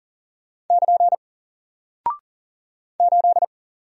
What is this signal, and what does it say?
Morse code, heard as a keyed tone, about 38 wpm, 700 Hz. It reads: <KN> 8
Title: Single Character (Letter:Number:Punctuation:Prosign) Alt 40wpm